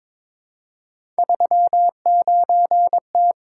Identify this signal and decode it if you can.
Morse code, heard as a keyed tone, about 22 wpm, 700 Hz. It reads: 39T
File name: practice / 22wpm / Three Letter-Number Combinations - 22wpm